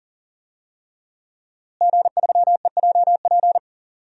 Morse code, heard as a keyed tone, about 40 wpm, 700 Hz. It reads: G3EJP
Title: DXCC-100 Call Signs - 5 Characters - Repeated 3x - 40wpm